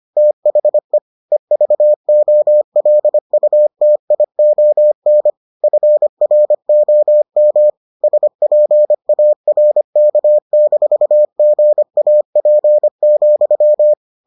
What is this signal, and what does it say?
Morse code, heard as a keyed tone, about 25 wpm, 600 Hz. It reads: THE EVOLUTION FROM SPARK-GAP,